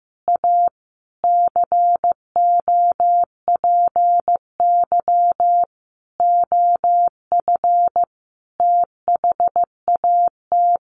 Morse code, heard as a keyed tone, about 15 wpm, 700 Hz. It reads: A COPY OF THAT
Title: Sentences from Top 1400 Words 15wpm